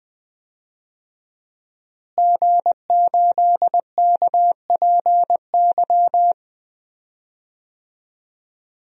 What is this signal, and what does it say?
Morse code, heard as a keyed tone, about 20 wpm, 700 Hz. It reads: G8KPY
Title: DXCC-100 Call Signs - 5 Characters - Repeated 2x - 20wpm